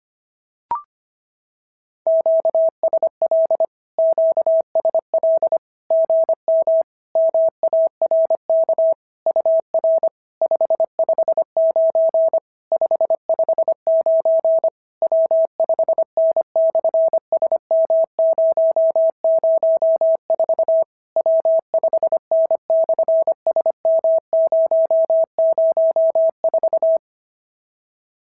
Morse code, heard as a keyed tone, about 25 wpm, 650 Hz. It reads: QSL QSL GM MARK UR 559 559 W5N/SM004 W5N/SM004
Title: SOTA QSO Practice 25wpm